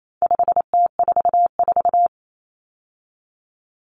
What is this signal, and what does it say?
Morse code, heard as a keyed tone, about 28 wpm, 700 Hz. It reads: 5T44
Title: Common Cut Numbers - 4-Digit 28wpm